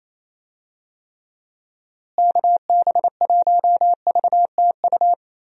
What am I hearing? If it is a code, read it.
Morse code, heard as a keyed tone, about 28 wpm, 700 Hz. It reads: KB1VTU
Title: USA Call Signs - 6 Characters - Repeated 3x - 28wpm